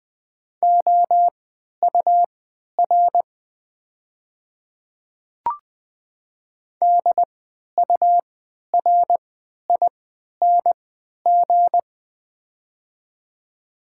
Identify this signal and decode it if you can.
Morse code, heard as a keyed tone, about 20 wpm, spaced as for 11 wpm, 700 Hz. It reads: OUR DURING
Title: 3x Character Spacing - Top 500 Words - 20wpm